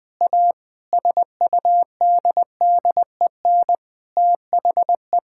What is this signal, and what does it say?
Morse code, heard as a keyed tone, about 20 wpm, 700 Hz. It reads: A SUDDEN THE
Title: Sentences with 6-Word Phrases - 20wpm